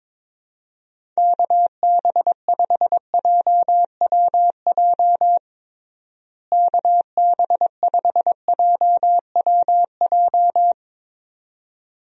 Morse code, heard as a keyed tone, about 22 wpm, 700 Hz. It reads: KB5JWJ KB5JWJ
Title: USA Call Signs - 6 Characters - Repeated 2x - 22wpm